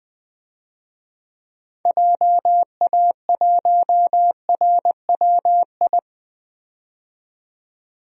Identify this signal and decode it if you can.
Morse code, heard as a keyed tone, about 21 wpm, 700 Hz. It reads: JA1RWI
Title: DXCC-100 Call Signs - 6 Characters - 20wpm